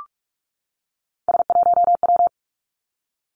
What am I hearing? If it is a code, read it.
Morse code, heard as a keyed tone, about 46 wpm, 700 Hz. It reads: S1W
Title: Three Challenging Letter-Number Combinations 45wpm